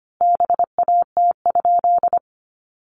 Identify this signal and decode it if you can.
Morse code, heard as a keyed tone, about 25 wpm, 700 Hz. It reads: BAT?
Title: Sentences from Top 1500 Words 25wpm